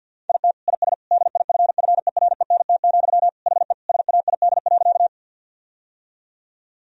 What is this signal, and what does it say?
Morse code, heard as a keyed tone, about 50 wpm, 700 Hz. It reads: IT IS DIFFERENT, HE SAID.
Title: Sentences from Top 1000 Words - Review Part 3 - 50wpm